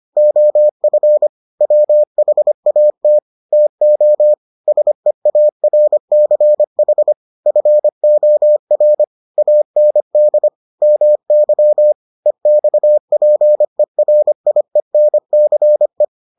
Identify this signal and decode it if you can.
Morse code, heard as a keyed tone, about 25 wpm, 600 Hz. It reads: OF WHAT TO SEARCH FOR AND MY EXPERIENCE